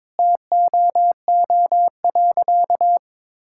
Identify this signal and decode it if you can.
Morse code, heard as a keyed tone, about 22 wpm, 700 Hz. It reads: TOO.
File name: practice / 22wpm / Sentences with 3-Word Phrases - 22wpm